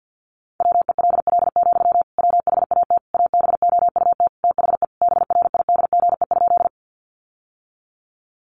Morse code, heard as a keyed tone, about 50 wpm, 700 Hz. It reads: WELL, WHAT ABOUT THE BRIDGE?
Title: Sentences from Top 1200 Words 50wpm